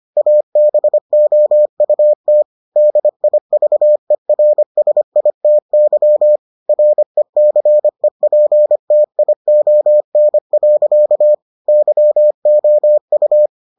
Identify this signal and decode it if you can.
Morse code, heard as a keyed tone, about 25 wpm, 600 Hz. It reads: ABOUT DIVERSITY RECEPTION. YOU